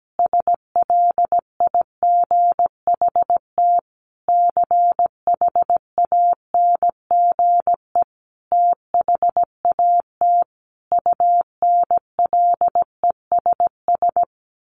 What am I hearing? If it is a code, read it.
Morse code, heard as a keyed tone, about 17 wpm, 700 Hz. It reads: SLIGHT CHANGE THAT UNLESS
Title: A Sound Of Thunder 17wpm